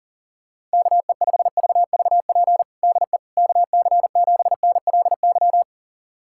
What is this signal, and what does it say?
Morse code, heard as a keyed tone, about 40 wpm, 700 Hz. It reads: KE5VVP DE KC7NLY
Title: USA Callsign DE Callsign 40wpm